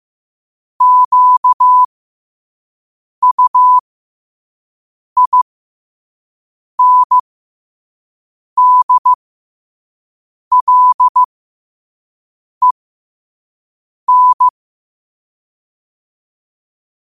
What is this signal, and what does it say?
Morse code, heard as a keyed tone, about 15 wpm, spaced as for 5 wpm, 1000 Hz. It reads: QUINDLEN
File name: qotdfiles / 5wpm